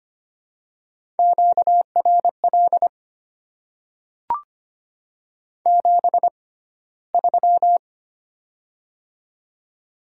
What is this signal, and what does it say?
Morse code, heard as a keyed tone, about 25 wpm, spaced as for 15 wpm, 700 Hz. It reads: QRL 73
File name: Top 20 QSO Elements - 6x Character Spacing - 25wpm